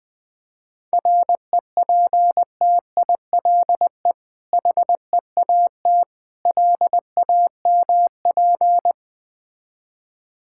Morse code, heard as a keyed tone, about 21 wpm, 700 Hz. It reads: REPTILE HEAT LAMP
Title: Difficult Words with Hints - 20wpm